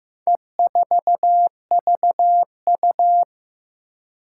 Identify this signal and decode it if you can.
Morse code, heard as a keyed tone, about 15 wpm, 700 Hz. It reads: E4VU
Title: Four Challenging Letter-Number Combinations 15wpm